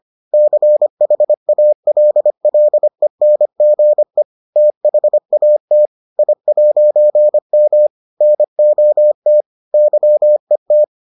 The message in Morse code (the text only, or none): CHALLENGE THAT I'M NOT YET